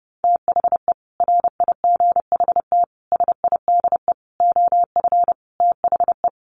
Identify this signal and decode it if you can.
Morse code, heard as a keyed tone, about 30 wpm, 700 Hz. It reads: THE RIGHT SIDE OF THE
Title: Sets of 5 Words Encore 30wpm